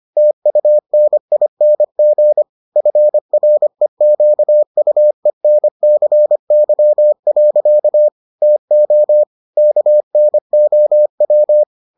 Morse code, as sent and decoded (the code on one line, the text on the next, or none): - ..- -. .. -. --. / ..-. .-. --.- ..- . -. -.-. -.-- .-.-.- / - --- / -.- -. --- .--
TUNING FREQUENCY. TO KNOW